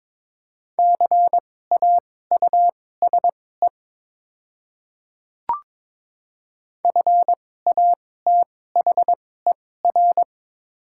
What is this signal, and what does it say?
Morse code, heard as a keyed tone, about 21 wpm, spaced as for 16 wpm, 700 Hz. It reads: CAUSE FATHER
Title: 2x Character Spacing - Top 200 Words - 22wpm